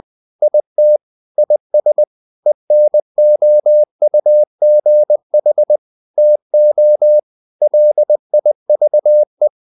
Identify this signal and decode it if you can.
Morse code, heard as a keyed tone, about 20 wpm, 600 Hz. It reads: IT IS ENOUGH TO LIVE
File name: the old man and the sea part 2 20wpm